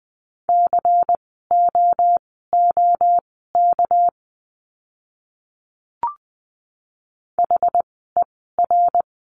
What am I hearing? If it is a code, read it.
Morse code, heard as a keyed tone, about 20 wpm, spaced as for 14 wpm, 700 Hz. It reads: COOK HER
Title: 199 Lesson 32f - Letter k - Words 20wpm